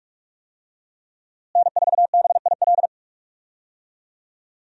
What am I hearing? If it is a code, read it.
Morse code, heard as a keyed tone, about 46 wpm, 700 Hz. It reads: N4BIL
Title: USA Call Signs - 5 Characters - Repeated 2x - 45wpm